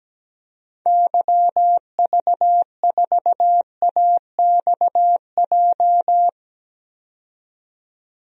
Morse code, heard as a keyed tone, about 17 wpm, 700 Hz. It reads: YV4AXJ
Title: DXCC-100 Call Signs - 6 Characters - 17wpm